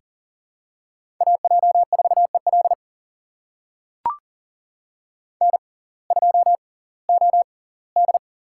Morse code, heard as a keyed tone, about 40 wpm, spaced as for 22 wpm, 700 Hz. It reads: AJ4EL N2OD
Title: USA Call Signs - 6x Character Spacing - 40wpm